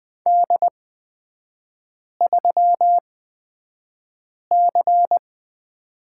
Morse code, heard as a keyed tone, about 20 wpm, spaced as for 5 wpm, 690 Hz. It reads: D3C
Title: 236 Lesson 38e - Letter q - Three Characters 20wpm